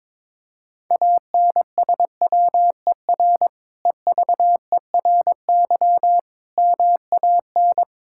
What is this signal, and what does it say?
Morse code, heard as a keyed tone, about 22 wpm, 700 Hz. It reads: ANSWER EVERY MAN
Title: Sentences from Top 1000 Words - Review Part 5 - 22wpm